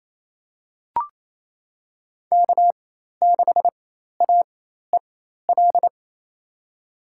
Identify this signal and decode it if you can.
Morse code, heard as a keyed tone, about 27 wpm, spaced as for 13 wpm, 700 Hz. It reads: K6AEL
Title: USA Call Signs - 4x Character Spacing - 28wpm